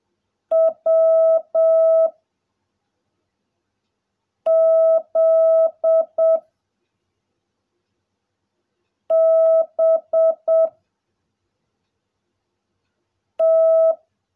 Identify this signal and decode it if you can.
Morse code, heard as a keyed tone, about 7 wpm, 660 Hz. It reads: W Z B T
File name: jh8cbh / 2021cw / cw16a